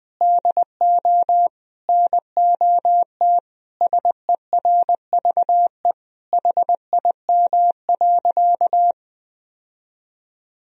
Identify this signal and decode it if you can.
Morse code, heard as a keyed tone, about 21 wpm, 700 Hz. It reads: DO NOT SERVE HIM.
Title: Sentences from Top 1000 Words - Review Part 2 - 20wpm